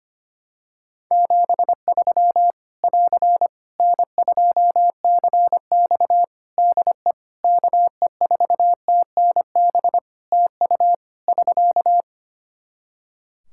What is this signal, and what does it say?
Morse code, heard as a keyed tone, about 25 wpm, 700 Hz. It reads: 73 <AR> N2CX DE KE4TNB TU <SK>
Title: Practice QSO - Exchange 4 - Ending - 25wpm